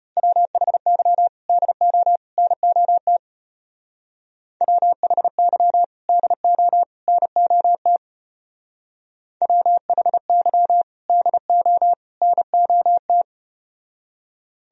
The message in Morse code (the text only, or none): WHY DO NOT WHY DO NOT WHY DO NOT